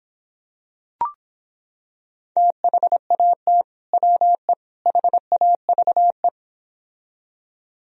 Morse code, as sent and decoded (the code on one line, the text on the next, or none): - .... .- - / .-- . / .... .- ...- .
THAT WE HAVE